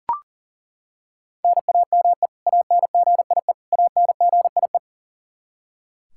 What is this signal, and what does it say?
Morse code, heard as a keyed tone, about 37 wpm, 700 Hz. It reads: NAME ANGIE ANGIE
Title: K1USN SST QSO Practice 40wpm